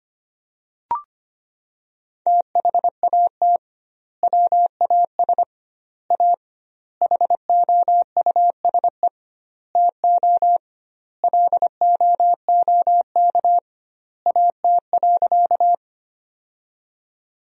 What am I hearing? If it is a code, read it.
Morse code, heard as a keyed tone, about 25 wpm, 700 Hz. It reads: THAT WAS A HOUSE TO LOOK AT.
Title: 2x Word-Spacing - Sentences from Top 200 Words - 25wpm